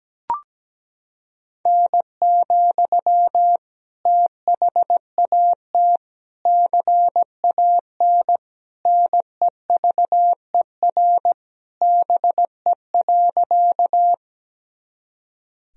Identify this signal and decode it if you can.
Morse code, heard as a keyed tone, about 17 wpm, 700 Hz. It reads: N, THAT CAN NEVER BE.
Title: Sentences from Top 300 Words and Abbreviations 17wpm